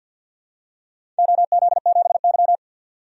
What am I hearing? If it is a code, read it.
Morse code, heard as a keyed tone, about 53 wpm, 700 Hz. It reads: OZ7Y
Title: DXCC-100 Call Signs - 4 Characters - 50wpm